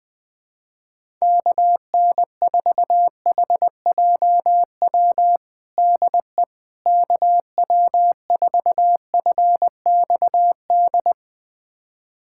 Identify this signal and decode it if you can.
Morse code, heard as a keyed tone, about 20 wpm, 700 Hz. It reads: KN4HJW DE KW4FXD
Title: USA Callsign DE Callsign - 6 Characters - 20wpm